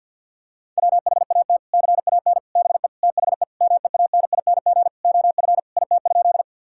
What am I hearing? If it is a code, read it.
Morse code, heard as a keyed tone, about 50 wpm, 700 Hz. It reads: WHAT CAN BE THE MEANING OF IT?
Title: Sentences from Top 1000 Words - Review Part 6 - 50wpm